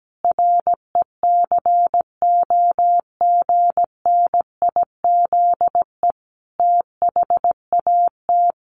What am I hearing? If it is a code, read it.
Morse code, heard as a keyed tone, about 17 wpm, 700 Hz. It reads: RECOGNIZE THAT
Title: Sentences from Top 1400 Words 17wpm